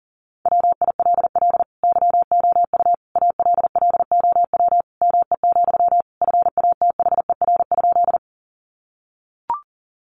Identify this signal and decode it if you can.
Morse code, heard as a keyed tone, about 40 wpm, 700 Hz. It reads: WILL YOU ALLOW ME, FATHER?
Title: Sentences from Top 700 Words 40wpm